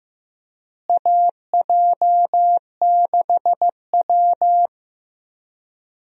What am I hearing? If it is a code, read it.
Morse code, heard as a keyed tone, about 15 wpm, 700 Hz. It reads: AJ6W
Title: USA Call Signs - 4 Characters - Repeated 2x - 15wpm